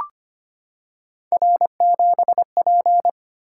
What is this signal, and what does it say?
Morse code, heard as a keyed tone, about 25 wpm, 700 Hz. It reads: R7P
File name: Three Letter-Number Combinations - 25wpm